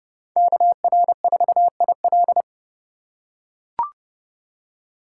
Morse code, heard as a keyed tone, about 30 wpm, 700 Hz. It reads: KR4IL
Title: USA Call Signs - 5 Characters - 3x Character Spacing - 30wpm